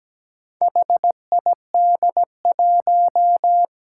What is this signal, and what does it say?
Morse code, heard as a keyed tone, about 17 wpm, 700 Hz. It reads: HID1
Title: Four Challenging Letter-Number Combinations 17wpm